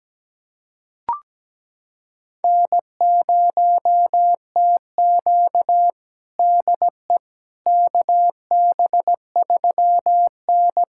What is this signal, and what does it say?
Morse code, heard as a keyed tone, about 17 wpm, 700 Hz. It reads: N0TQ DE KB3N
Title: USA Callsign DE Callsign - 4 Characters - 17wpm